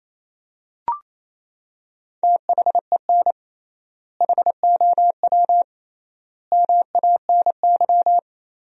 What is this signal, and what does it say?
Morse code, heard as a keyed tone, about 28 wpm, 700 Hz. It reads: THEN HOW MANY
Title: 3x Word-Spacing - Sentences from Top 100 Words - 28wpm